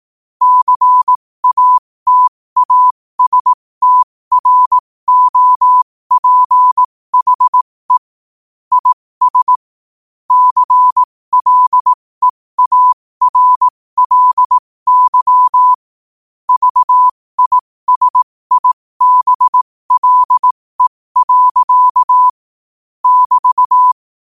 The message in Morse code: CATASTROPHE IS CLEARLY VISIBLE. <BT>

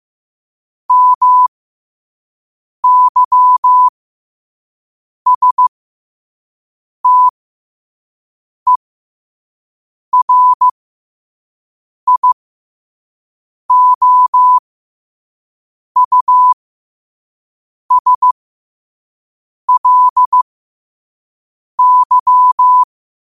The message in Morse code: MYSTERIOUSLY

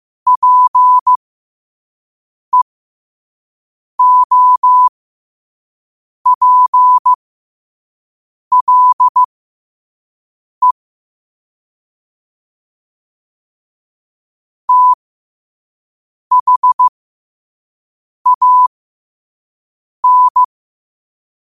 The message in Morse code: PEOPLE THAN

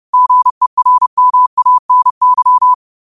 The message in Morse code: GERMANY

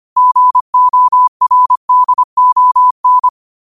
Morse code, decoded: GORDON

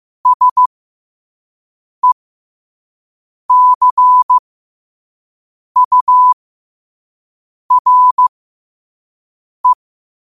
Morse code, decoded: SECURE

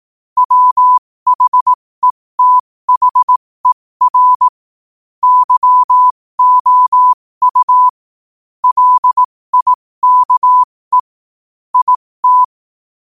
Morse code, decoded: WHETHER YOU LIKE IT

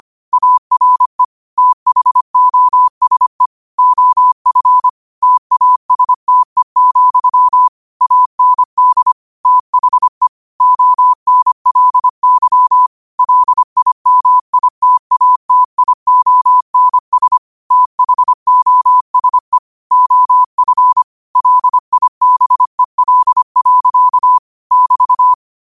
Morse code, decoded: ARE THOSE OF TASTE, AND THE ONLY LIMITATIONS THOSE OF LIBEL. <BT>

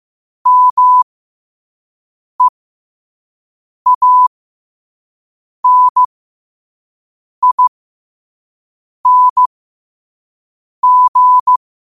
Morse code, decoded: MEANING

